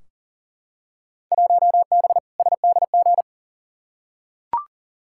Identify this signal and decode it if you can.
Morse code, heard as a keyed tone, about 40 wpm, 700 Hz. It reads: 1B SDG
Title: Field Day Exchanges Encore - 40wpm